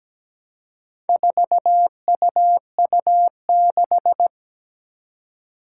Morse code, heard as a keyed tone, about 17 wpm, 700 Hz. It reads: 4UU6
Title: Four Challenging Letter-Number Combinations 17wpm